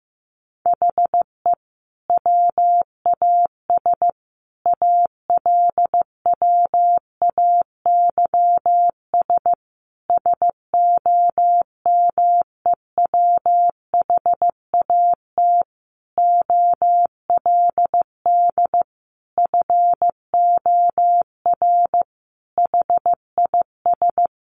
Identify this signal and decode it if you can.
Morse code, heard as a keyed tone, about 15 wpm, 700 Hz. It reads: HE WAS ALWAYS SOMEWHAT OLD FOR HIS
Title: Sentences from Top 1300 Words 15wpm